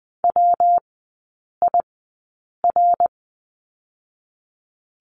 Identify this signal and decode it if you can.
Morse code, heard as a keyed tone, about 20 wpm, 700 Hz. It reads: W I R
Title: 114 Lesson 19e - Letter w - Three Characters 20wpm